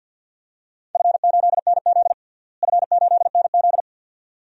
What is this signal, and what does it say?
Morse code, heard as a keyed tone, about 47 wpm, 700 Hz. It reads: F8NZ F8NZ
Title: DXCC-100 Call Signs - 4 Characters - Repeated 2x - 50wpm